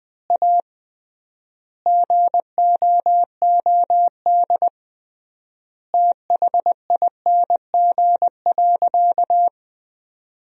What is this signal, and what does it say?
Morse code, heard as a keyed tone, about 20 wpm, 700 Hz. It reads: A GOOD THING.
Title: 3x Word-Spacing - Sentences from Top 300 Words - 20wpm